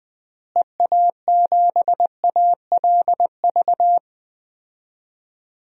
Morse code, heard as a keyed tone, about 21 wpm, 700 Hz. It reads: EA7ALV
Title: DXCC-100 Call Signs - 6 Characters - Repeated 3x - 20wpm